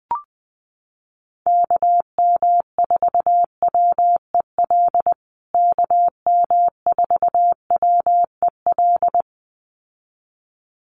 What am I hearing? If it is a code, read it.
Morse code, heard as a keyed tone, about 19 wpm, 700 Hz. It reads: KM4WEL KM4WEL
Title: USA Call Signs Repeated 2x 20wpm